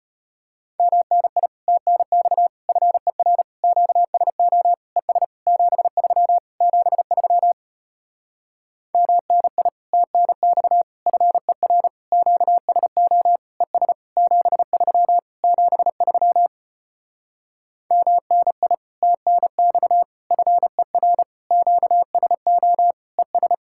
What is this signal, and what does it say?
Morse code, heard as a keyed tone, about 33 wpm, 700 Hz. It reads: MNI TNX FER QSO ES 73 73 MNI TNX FER QSO ES 73 73 MNI TNX FER QSO ES